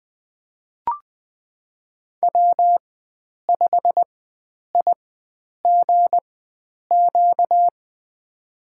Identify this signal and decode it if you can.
Morse code, heard as a keyed tone, about 20 wpm, spaced as for 9 wpm, 700 Hz. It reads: W5IGQ
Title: USA Call Signs - 5 Characters - 4x Character Spacing - 20wpm